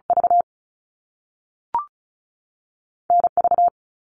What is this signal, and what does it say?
Morse code, heard as a keyed tone, about 36 wpm, 700 Hz. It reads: V NV
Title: Top Two Letter Combinations in Words 35wpm